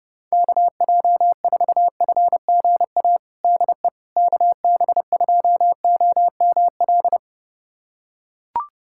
Morse code, heard as a keyed tone, about 32 wpm, 700 Hz. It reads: KJ4FGA DE KB2OML
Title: USA Callsign DE Callsign - 6 Characters - 30wpm